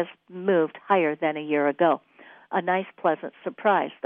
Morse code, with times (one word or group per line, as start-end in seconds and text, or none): none